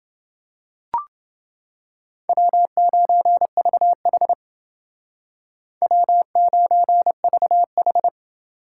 Morse code, heard as a keyed tone, about 28 wpm, 700 Hz. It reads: W9VH W9VH